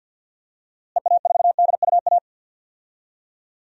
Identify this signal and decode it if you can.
Morse code, heard as a keyed tone, about 50 wpm, 700 Hz. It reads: EA4DUA
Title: DXCC-100 Call Signs - 6 Characters - Repeated 3x - 50wpm